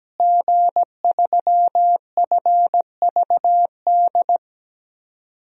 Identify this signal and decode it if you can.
Morse code, heard as a keyed tone, about 17 wpm, 700 Hz. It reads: G3FVD
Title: DXCC-100 Call Signs - 5 Characters - Repeated 3x - 17wpm